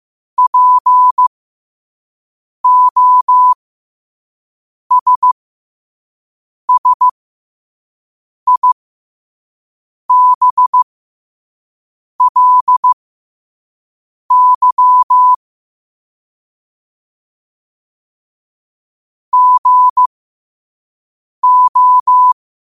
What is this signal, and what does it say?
Morse code, heard as a keyed tone, about 15 wpm, spaced as for 5 wpm, 1000 Hz. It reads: POSSIBLY GO
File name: qotdfiles / 5wpm